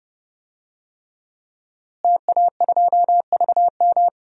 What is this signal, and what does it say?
Morse code, heard as a keyed tone, about 30 wpm, 700 Hz. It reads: TA2VM
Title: DXCC-100 Call Signs - 5 Characters - Repeated 3x - 30wpm